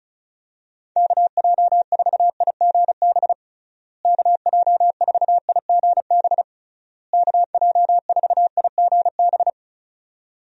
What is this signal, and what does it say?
Morse code, heard as a keyed tone, about 36 wpm, 700 Hz. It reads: KJ4IGB KJ4IGB KJ4IGB
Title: USA Call Signs - 6 Characters - Repeated 3x - 35wpm